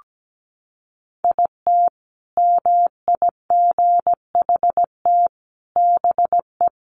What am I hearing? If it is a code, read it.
Morse code, heard as a keyed tone, about 17 wpm, 700 Hz. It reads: IT MIGHT BE
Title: Sentences from Top 1200 Words 17wpm